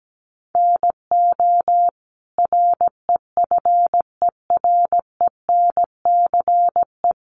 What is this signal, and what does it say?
Morse code, heard as a keyed tone, about 17 wpm, 700 Hz. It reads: NO REFERENCE
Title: Sentences from Top 1200 Words 17wpm